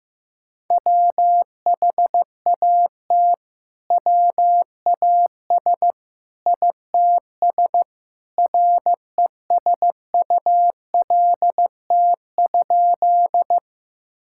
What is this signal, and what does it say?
Morse code, heard as a keyed tone, about 16 wpm, 700 Hz. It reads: WHAT WAS ITS RESULT?
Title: Sentences from Top 1000 Words - Review Part 5 - 15wpm